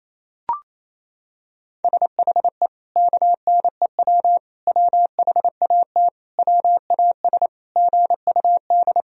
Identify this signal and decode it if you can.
Morse code, heard as a keyed tone, about 28 wpm, 700 Hz. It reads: SHE KNEW WHAT WAS GUD
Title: Sentences from Top 400 Words and Abbreviations 28wpm